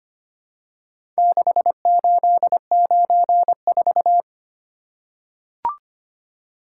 Morse code, heard as a keyed tone, about 25 wpm, 700 Hz. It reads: 6894
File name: practice / 25wpm / Four Letter-Number Combinations - 25wpm